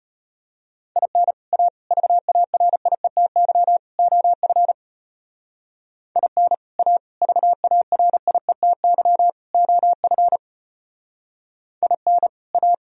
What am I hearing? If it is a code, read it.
Morse code, heard as a keyed tone, about 31 wpm, 700 Hz. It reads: IN A VARIETY OF IN A VARIETY OF IN A
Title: Speed-Racing - Sets of 4 Words 25wpm